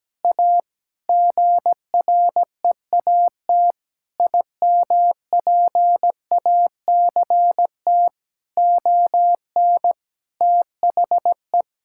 Morse code, heard as a keyed tone, about 17 wpm, 700 Hz. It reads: A GREAT IMPACT ON THE
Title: Sentences from Top 1000 Words - Review Part 6 - 17wpm